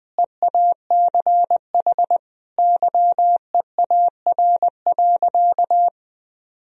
Morse code, heard as a keyed tone, about 20 wpm, 700 Hz. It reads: EACH YEAR.
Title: Sentences from Top 1000 Words - Review Part 4 - 20wpm